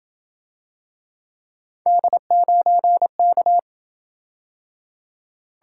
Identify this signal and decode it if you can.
Morse code, heard as a keyed tone, about 28 wpm, 700 Hz. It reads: D9K